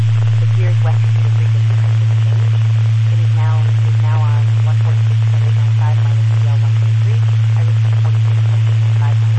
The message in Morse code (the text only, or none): none